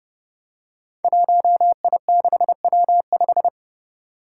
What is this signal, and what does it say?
Morse code, heard as a keyed tone, about 30 wpm, 700 Hz. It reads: 1I6W5
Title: Five Challenging Letter-Number Combinations 30wpm